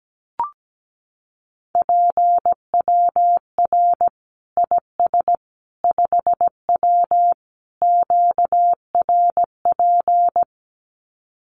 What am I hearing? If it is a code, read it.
Morse code, heard as a keyed tone, about 17 wpm, 700 Hz. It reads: PWR IS 5W QRP
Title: Common QSO Phrases 17wpm